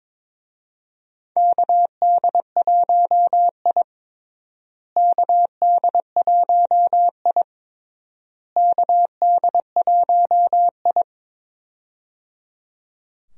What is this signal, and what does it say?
Morse code, heard as a keyed tone, about 21 wpm, 700 Hz. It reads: KD1I KD1I KD1I
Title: USA Call Signs - 4 Characters - Repeated 3x - 22wpm